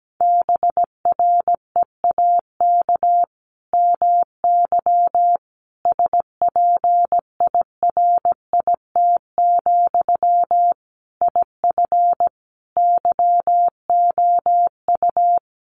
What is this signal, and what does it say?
Morse code, heard as a keyed tone, about 17 wpm, 700 Hz. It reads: BREAK MY SPIRIT, IF YOU
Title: Sentences from Top 1400 Words 17wpm